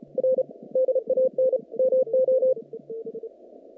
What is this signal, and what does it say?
Morse code, heard as a keyed tone, about 34 wpm, 520 Hz. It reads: R DAN MO
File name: contests / naqp-cw-k5tr-op-k5pi- -2200